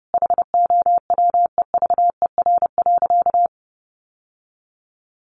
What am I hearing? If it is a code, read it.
Morse code, heard as a keyed tone, about 30 wpm, 700 Hz. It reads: HOWEVER.
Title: Sentences from Top 800 Words 30wpm